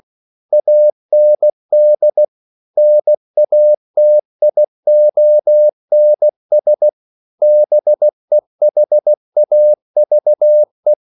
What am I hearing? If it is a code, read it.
Morse code, heard as a keyed tone, about 16 wpm, 600 Hz. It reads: AND NATIONS BEHAVE
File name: CW / Practice / EnglishQuotations0416